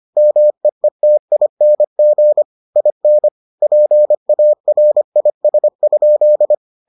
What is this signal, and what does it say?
Morse code, heard as a keyed tone, about 25 wpm, 600 Hz. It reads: MEETING IN PARIS?